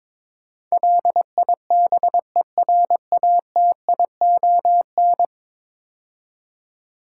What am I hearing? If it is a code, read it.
Morse code, heard as a keyed tone, about 22 wpm, 700 Hz. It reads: LIBERATION
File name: practice / 22wpm / 10 Letter Words - First 3 Letters Given - 22wpm